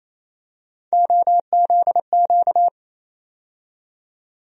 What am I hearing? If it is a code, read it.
Morse code, heard as a keyed tone, about 28 wpm, 700 Hz. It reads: OZQ
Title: USA Call Sign Postfixes - 3 Letters 28wpm